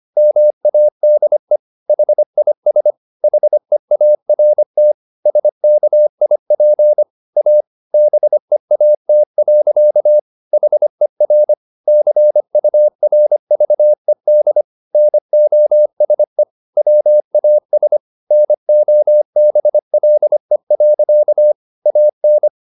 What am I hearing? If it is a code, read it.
Morse code, heard as a keyed tone, about 25 wpm, 600 Hz. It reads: MADE HIS HEART SKIP A BEAT. HER CURVED NOSE WAS NOBLE. AN